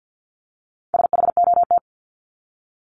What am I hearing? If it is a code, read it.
Morse code, heard as a keyed tone, about 50 wpm, 700 Hz. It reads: SHOT
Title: Top 1001-1250 Words 50wpm